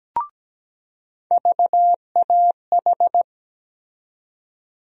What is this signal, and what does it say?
Morse code, heard as a keyed tone, about 17 wpm, 700 Hz. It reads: VAH
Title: USA Call Sign Postfixes - 3 Letters 17wpm